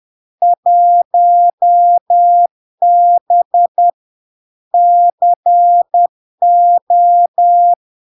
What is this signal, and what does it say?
Morse code, heard as a keyed tone, about 10 wpm, 700 Hz. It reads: 1B CO
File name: LARK-FD-2022-Log-in-CW-10wpm-700-Hz